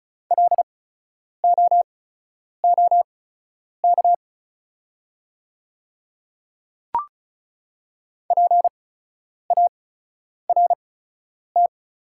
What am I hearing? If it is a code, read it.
Morse code, heard as a keyed tone, about 36 wpm, spaced as for 10 wpm, 700 Hz. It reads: LOOK PART
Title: 8x Character Spacing - Top 200 Words - 35wpm